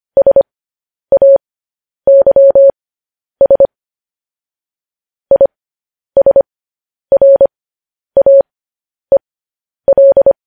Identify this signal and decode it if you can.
Morse code, heard as a keyed tone, about 25 wpm, spaced as for 10 wpm, 550 Hz. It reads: SAYS ISRAEL
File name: podcast / News headlines-1x-25x10